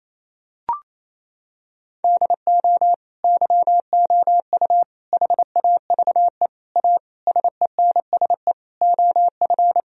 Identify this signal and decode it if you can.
Morse code, heard as a keyed tone, about 28 wpm, 700 Hz. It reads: DO YOU HAVE A SENSE OF
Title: Sentences from Top 1000 Words - Review Part 4 - 28wpm